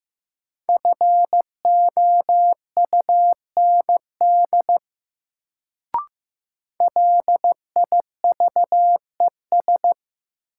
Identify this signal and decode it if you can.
Morse code, heard as a keyed tone, about 15 wpm, 700 Hz. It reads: FOUND LIVES
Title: Sets of 5 Words Encore - New Word Drill - 15wpm